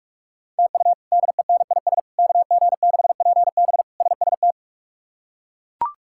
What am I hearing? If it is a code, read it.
Morse code, heard as a keyed tone, about 42 wpm, 700 Hz. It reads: TU DENIS KG6PB SST
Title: K1USN SST QSO Practice - 45wpm